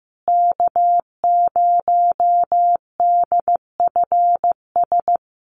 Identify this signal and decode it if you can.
Morse code, heard as a keyed tone, about 15 wpm, 700 Hz. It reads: K0DFS
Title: USA Call Signs Repeated 3x 15wpm